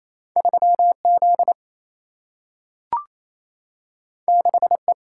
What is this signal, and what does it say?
Morse code, heard as a keyed tone, about 28 wpm, 700 Hz. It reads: 3 Z 6 E
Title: Single-Letter Single-Number 28wpm